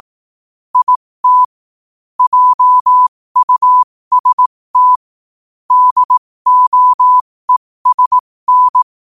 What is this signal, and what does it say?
Morse code, heard as a keyed tone, about 18 wpm, spaced as for 15 wpm, 1000 Hz. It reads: IT JUST DOESN
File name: qotdfiles / 15wpm